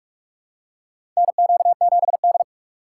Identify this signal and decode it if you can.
Morse code, heard as a keyed tone, about 46 wpm, 700 Hz. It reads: NQ7D
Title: USA Call Signs - 4 Characters - 3x Character Spacing - 45wpm